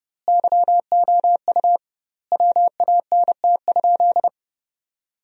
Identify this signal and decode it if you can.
Morse code, heard as a keyed tone, about 32 wpm, 700 Hz. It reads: YOU WANT?
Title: 2x Word-Spacing - Sentences from Top 200 Words - 30wpm